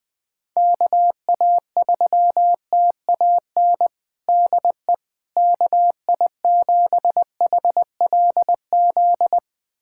Morse code, heard as a keyed tone, about 21 wpm, 700 Hz. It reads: KA3TAN DE KI7HLZ
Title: USA Callsign DE Callsign - 6 Characters - 20wpm